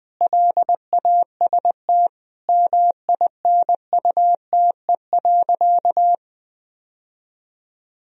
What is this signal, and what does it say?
Morse code, heard as a keyed tone, about 20 wpm, 700 Hz. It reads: LAST MINUTE.
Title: Sentences from Top 1000 Words - Review Part 2 - 20wpm